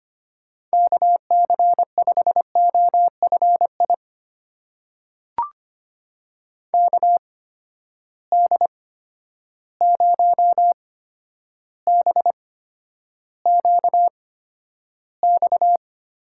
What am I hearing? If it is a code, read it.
Morse code, heard as a keyed tone, about 25 wpm, spaced as for 11 wpm, 700 Hz. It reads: KC5OFI KD0BQX
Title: USA Call Signs - 6 Characters - 8x Character Spacing - 25wpm